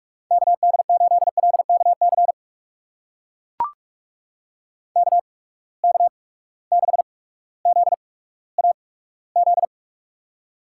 Morse code, heard as a keyed tone, about 44 wpm, spaced as for 19 wpm, 700 Hz. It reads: KD8LKC KK6ZAZ
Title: USA Call Signs - 6 Characters - 8x Character Spacing - 45wpm